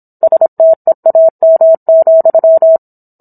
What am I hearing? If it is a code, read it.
Morse code, heard as a keyed tone, about 27 wpm, 650 Hz. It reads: STEAM,